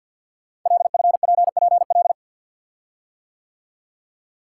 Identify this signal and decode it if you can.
Morse code, heard as a keyed tone, about 50 wpm, 700 Hz. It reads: LFPPL
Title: Code Groups - 5 Letters - 50wpm